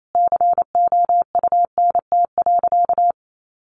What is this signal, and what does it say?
Morse code, heard as a keyed tone, about 28 wpm, 700 Hz. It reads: COUNT.
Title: Sentences from Top 1500 Words 28wpm